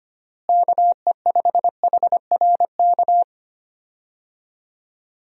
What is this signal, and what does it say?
Morse code, heard as a keyed tone, about 25 wpm, 700 Hz. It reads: KE5HRK
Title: USA Call Signs - 6 Characters - Repeated 2x - 25wpm